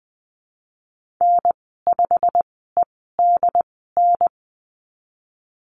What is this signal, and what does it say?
Morse code, heard as a keyed tone, about 20 wpm, spaced as for 15 wpm, 700 Hz. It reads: N5EDN
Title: 088 Lesson 15g - Number 5 - Call Signs 20wpm